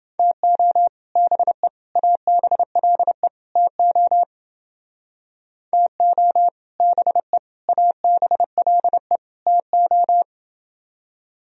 Morse code, heard as a keyed tone, about 28 wpm, 700 Hz. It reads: TO BE ABLE TO TO BE ABLE TO